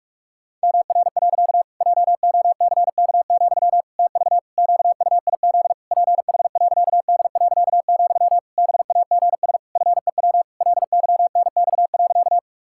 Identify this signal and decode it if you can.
Morse code, heard as a keyed tone, about 43 wpm, 700 Hz. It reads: MR. JOCK, TV QUIZ PH.D., BAGS FEW LYNX.
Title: Pangram Challenge v2 - 45wpm